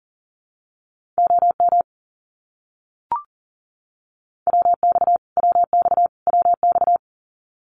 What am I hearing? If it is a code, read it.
Morse code, heard as a keyed tone, about 40 wpm, 700 Hz. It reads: OM WX WX WX
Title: Abbreviations Repeated 3x - 40wpm